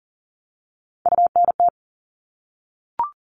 Morse code, heard as a keyed tone, about 40 wpm, 700 Hz. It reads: UNT
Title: All Cut Numbers - 3-Digit 40wpm